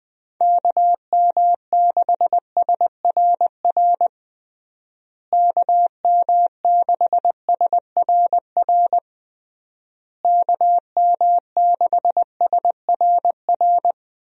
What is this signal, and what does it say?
Morse code, heard as a keyed tone, about 20 wpm, 700 Hz. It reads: KM6SRR KM6SRR KM6SRR
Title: USA Call Signs - 6 Characters - Repeated 3x - 20wpm